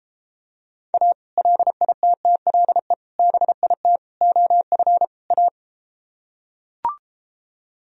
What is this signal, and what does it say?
Morse code, heard as a keyed tone, about 33 wpm, 700 Hz. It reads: A LITTLE BIT OF A